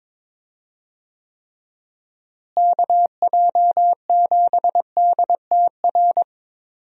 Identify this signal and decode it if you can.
Morse code, heard as a keyed tone, about 22 wpm, 700 Hz. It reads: KJ7DTR
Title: USA Call Signs - 6 Characters - Repeated 3x - 22wpm